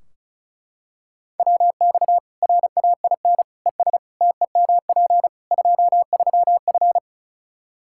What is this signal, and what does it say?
Morse code, heard as a keyed tone, about 34 wpm, 700 Hz. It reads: WX RAIN ES TEMP 23F
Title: K1USN SST QSO Practice 35wpm